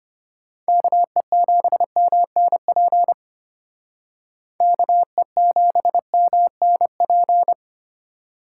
Morse code, heard as a keyed tone, about 31 wpm, 700 Hz. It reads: KE7MNP KE7MNP
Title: Speed Racing - USA Call Signs 25wpm